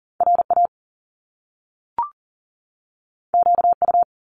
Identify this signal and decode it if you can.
Morse code, heard as a keyed tone, about 40 wpm, 700 Hz. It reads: RA QU
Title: Top Two Letter Combinations in Words 40wpm